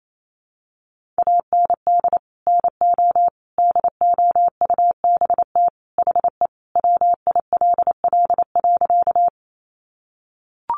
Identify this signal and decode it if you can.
Morse code, heard as a keyed tone, about 28 wpm, 700 Hz. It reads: AND NO DOUBT HE WILL.
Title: Sentences from Top 1500 Words 28wpm